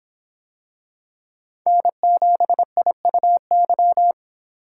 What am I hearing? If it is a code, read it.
Morse code, heard as a keyed tone, about 25 wpm, 700 Hz. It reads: N7IUY